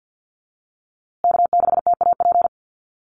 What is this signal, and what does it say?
Morse code, heard as a keyed tone, about 50 wpm, 700 Hz. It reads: K6TAP
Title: USA Call Signs Repeated 2x 50wpm